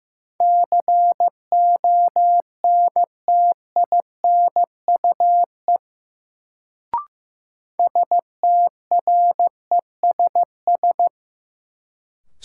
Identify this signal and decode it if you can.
Morse code, heard as a keyed tone, about 15 wpm, 700 Hz. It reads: CONTINUE STRESS
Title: Sentences from Top 1000 Words - Review Part 6 - New Word Drill - 15wpm